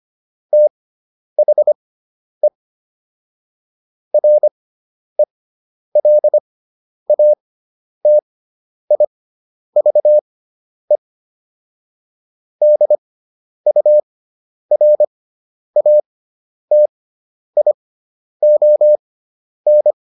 Morse code, heard as a keyed tone, about 25 wpm, spaced as for 10 wpm, 600 Hz. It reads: THE RELATIVE DURATION